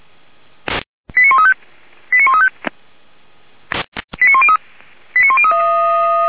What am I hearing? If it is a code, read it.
Five-tone selective call (ZVEI), heard as quick runs of five tones, tones of about 70 ms. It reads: 89135 89135 89103